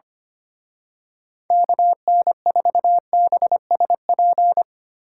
Morse code, heard as a keyed tone, about 26 wpm, 700 Hz. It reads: KN4BSP
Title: Speed Racing - USA 2x3 Call Signs 25wpm